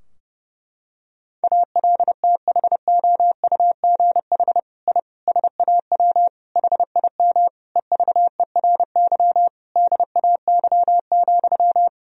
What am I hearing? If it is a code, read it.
Morse code, heard as a keyed tone, about 30 wpm, 700 Hz. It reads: ALTHOUGH I SAW HIM EVERY DAY,
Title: Sentences from Top 1000 Words - Review Part 3 - 30wpm